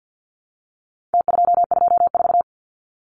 Morse code, heard as a keyed tone, about 49 wpm, 700 Hz. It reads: T224
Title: Numbers - Contest Style - 4-Digit 50wpm